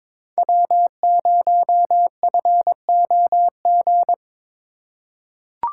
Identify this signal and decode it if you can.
Morse code, heard as a keyed tone, about 22 wpm, 700 Hz. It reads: W0FOG
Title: USA Call Signs - 5 Characters - 4x Character Spacing - 22wpm